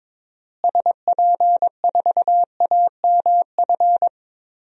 Morse code, heard as a keyed tone, about 22 wpm, 700 Hz. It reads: SP4AMF
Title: DXCC-100 Call Signs - 6 Characters - Repeated 2x - 22wpm